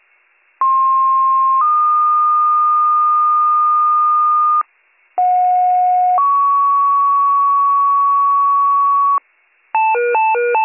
Two-tone page, heard as a one-second tone, then a three-second tone, about 1060 then 1220 Hz, about 730 then 1120 Hz. It is heard from 0.6 to 9.2 s.